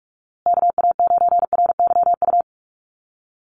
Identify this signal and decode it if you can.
Morse code, heard as a keyed tone, about 44 wpm, 700 Hz. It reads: KA9RYU
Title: USA Call Signs Repeated 2x 45wpm